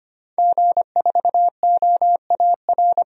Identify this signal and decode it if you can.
Morse code, heard as a keyed tone, about 25 wpm, 700 Hz. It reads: G4OAR
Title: DXCC-100 Call Signs - 5 Characters - 25wpm